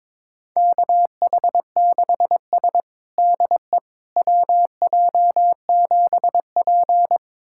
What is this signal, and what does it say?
Morse code, heard as a keyed tone, about 22 wpm, 700 Hz. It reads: KH6S DE WJ7P
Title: USA Callsign DE Callsign - 4 Characters - 22wpm